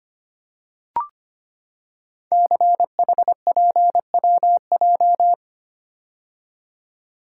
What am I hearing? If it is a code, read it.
Morse code, heard as a keyed tone, about 25 wpm, 700 Hz. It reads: CHPWJ
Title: Code Groups - 5 Letters - 25wpm